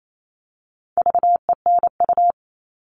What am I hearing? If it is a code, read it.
Morse code, heard as a keyed tone, about 28 wpm, 700 Hz. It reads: VENU
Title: All Cut Numbers - 4-Digit 28wpm